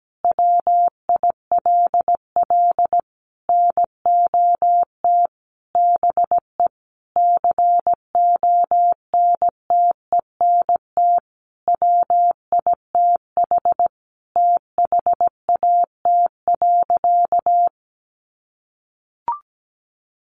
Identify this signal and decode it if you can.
Morse code, heard as a keyed tone, about 17 wpm, 700 Hz. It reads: WILL NOT BE CONTENT WITH THAT.
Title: Sentences from Top 900 Words 17wpm